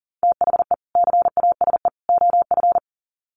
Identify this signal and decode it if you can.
Morse code, heard as a keyed tone, about 40 wpm, 700 Hz. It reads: THE CASE OF
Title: Sets of 3 Words 40wpm